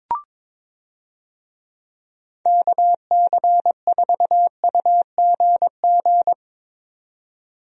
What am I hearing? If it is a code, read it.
Morse code, heard as a keyed tone, about 22 wpm, 700 Hz. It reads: KC4UGG
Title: USA Call Signs - 6 Characters - 22wpm